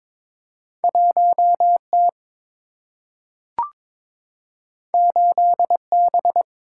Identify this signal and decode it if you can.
Morse code, heard as a keyed tone, about 22 wpm, 700 Hz. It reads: 1 T 8 B
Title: Single-Letter Single-Number 22wpm